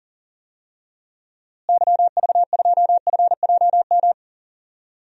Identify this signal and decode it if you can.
Morse code, heard as a keyed tone, about 40 wpm, 700 Hz. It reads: YV2FJM
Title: DXCC-100 Call Signs - 6 Characters - Repeated 2x - 40wpm